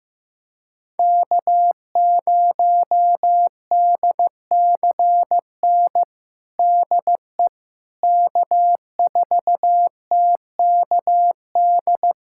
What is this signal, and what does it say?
Morse code, heard as a keyed tone, about 15 wpm, 700 Hz. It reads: K0DCN DE K4TKD
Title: USA Callsign DE Callsign - 5 Characters - 15wpm